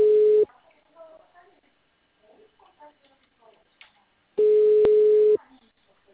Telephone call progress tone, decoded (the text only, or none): ringback tone